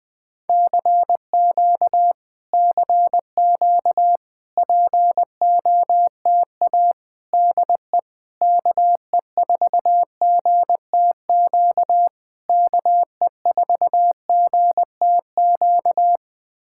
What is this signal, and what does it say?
Morse code, heard as a keyed tone, about 20 wpm, 700 Hz. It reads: CQ CQ POTA DE KE4GTQ KE4GTQ
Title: POTA QSO Practice - 20wpm